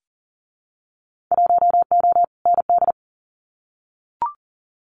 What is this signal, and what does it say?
Morse code, heard as a keyed tone, about 40 wpm, 700 Hz. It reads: 1O ND
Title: Winter Field Day Exchanges 40wpm